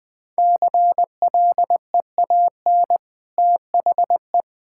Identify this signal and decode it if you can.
Morse code, heard as a keyed tone, about 20 wpm, 700 Hz. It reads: CLEAN THE
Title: Easy Sentences with Hints - 20wpm